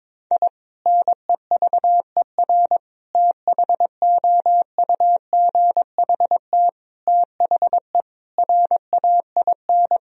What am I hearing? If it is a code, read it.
Morse code, heard as a keyed tone, about 21 wpm, 700 Hz. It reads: I NEVER THOUGHT THE RAIN
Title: Sentences from Top 1000 Words - Review Part 1 - 22wpm